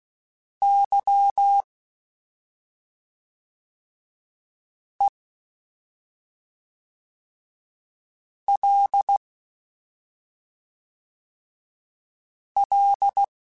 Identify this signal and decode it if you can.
Morse code, heard as a keyed tone, about 16 wpm, 800 Hz. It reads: Y E L L
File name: Special INGs / Special INGs-mp3s / SlowToFaster ing @5